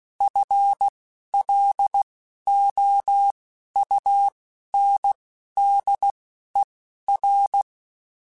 Morse code, heard as a keyed tone, about 16 wpm, spaced as for 11 wpm, 800 Hz. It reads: FLOUNDER